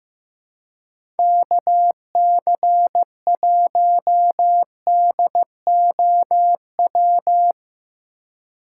Morse code, heard as a keyed tone, about 15 wpm, 700 Hz. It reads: KC1DOW
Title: USA Call Signs - 6 Characters - 2x Character Spacing - 15wpm